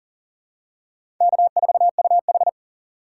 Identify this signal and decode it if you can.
Morse code, heard as a keyed tone, about 40 wpm, 700 Hz. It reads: K4UH
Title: USA Call Signs - 4 Characters - 4x Character Spacing - 40wpm